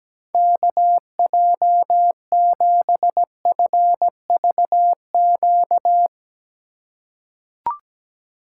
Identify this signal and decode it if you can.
Morse code, heard as a keyed tone, about 17 wpm, 700 Hz. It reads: KJ7FVQ
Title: USA Call Signs - 6 Characters - 3x Character Spacing - 17wpm